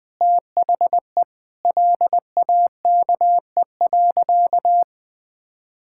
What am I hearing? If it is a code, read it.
Morse code, heard as a keyed tone, about 20 wpm, 700 Hz. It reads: THE LAKE.
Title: Sentences with 6-Word Phrases - 20wpm